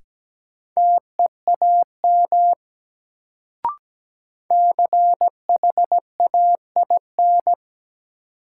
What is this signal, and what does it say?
Morse code, heard as a keyed tone, about 17 wpm, 700 Hz. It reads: TEAM CHAIN
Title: Sentences from Top 1500 Words - New Word Drill Alt - 17wpm